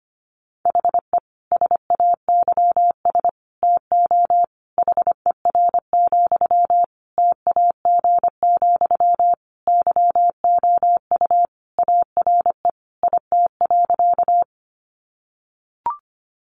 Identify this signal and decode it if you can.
Morse code, heard as a keyed tone, about 25 wpm, 700 Hz. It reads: HE SAYS TO HER, TAG, YOU ARE IT.
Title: Additional Easy Phrases 25wpm